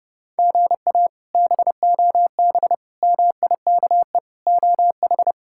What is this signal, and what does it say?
Morse code, heard as a keyed tone, about 29 wpm, 700 Hz. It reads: GA BOB MIKE OH
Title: K1USN SST QSO Practice - 30wpm